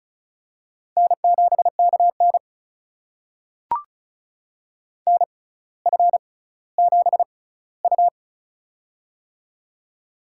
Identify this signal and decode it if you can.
Morse code, heard as a keyed tone, about 36 wpm, spaced as for 18 wpm, 700 Hz. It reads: N7KN NF7U
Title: USA Call Signs - 4 Characters - 6x Character Spacing - 35wpm